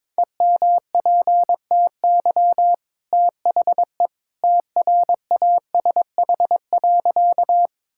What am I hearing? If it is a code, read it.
Morse code, heard as a keyed tone, about 22 wpm, 690 Hz. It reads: EMPTY THE TRASH.
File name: practice / 22wpm / Easy Sentences with Hints - 22wpm